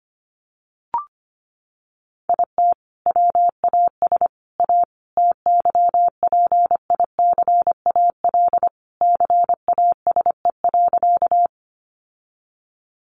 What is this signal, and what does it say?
Morse code, heard as a keyed tone, about 25 wpm, 700 Hz. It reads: IT WAS A TYPICAL CASE.
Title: Sentences from Top 1100 Words 25wpm